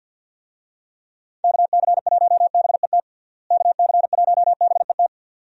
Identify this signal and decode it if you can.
Morse code, heard as a keyed tone, about 52 wpm, 700 Hz. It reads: KC1BET KC1BET
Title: USA Call Signs - 6 Characters - Repeated 2x - 50wpm